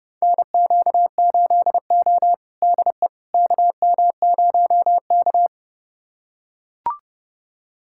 Morse code, heard as a keyed tone, about 30 wpm, 700 Hz. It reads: NQ8O DE KM0K
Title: USA Callsign DE Callsign - 4 Characters - 30wpm